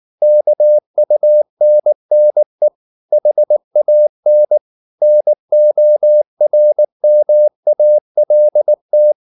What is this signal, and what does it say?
Morse code, heard as a keyed tone, about 19 wpm, 600 Hz. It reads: KUNNE HAN NORMALT
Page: Ferie 19 wpm